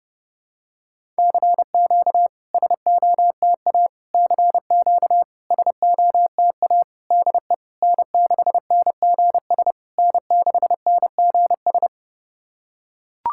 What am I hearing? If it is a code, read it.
Morse code, heard as a keyed tone, about 30 wpm, 700 Hz. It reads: CQ SOTA CQ SOTA DE N6NGS N6NGS
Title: SOTA QSO Practice 30wpm